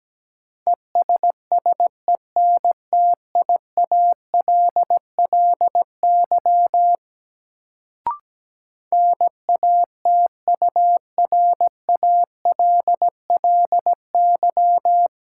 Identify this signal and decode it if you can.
Morse code, heard as a keyed tone, about 17 wpm, 700 Hz. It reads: ESSENTIALLY NATURALLY
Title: Sentences from Top 1300 Words - New Word Drill - 17wpm